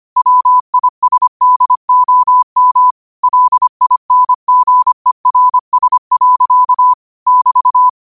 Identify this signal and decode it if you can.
Morse code, heard as a keyed tone, about 25 wpm, 1000 Hz. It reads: WISDOM LINGERS. <BT>